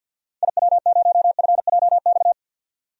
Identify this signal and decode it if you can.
Morse code, heard as a keyed tone, about 47 wpm, 700 Hz. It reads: IW0FJX
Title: DXCC-100 Call Signs - 6 Characters - 50wpm